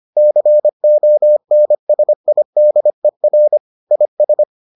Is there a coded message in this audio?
Yes: Morse code from 0.2 to 4.4 s, heard as a keyed tone, about 25 wpm, 600 Hz. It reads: CONSIDER IS